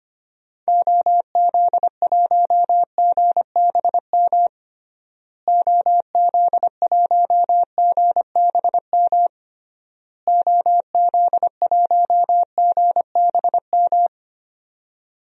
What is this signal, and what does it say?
Morse code, heard as a keyed tone, about 25 wpm, 700 Hz. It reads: OZ1GBM OZ1GBM OZ1GBM
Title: DXCC-100 Call Signs - 6 Characters - Repeated 3x - 25wpm